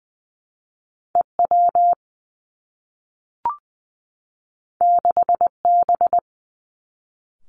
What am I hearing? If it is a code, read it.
Morse code, heard as a keyed tone, about 20 wpm, 700 Hz. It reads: E W 6 B
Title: Two Challenging Letter-Number Combinations 20wpm